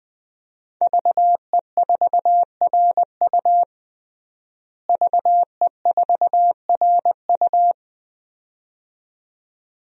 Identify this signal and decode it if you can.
Morse code, heard as a keyed tone, about 20 wpm, 700 Hz. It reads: VE4RU VE4RU
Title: DXCC-100 Call Signs - 5 Characters - Repeated 2x - 20wpm